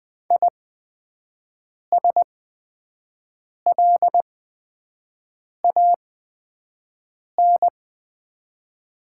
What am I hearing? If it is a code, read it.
Morse code, heard as a keyed tone, about 20 wpm, spaced as for 5 wpm, 690 Hz. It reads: ISLAN
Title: 8x Character Spacing - Top 500 Words - 20wpm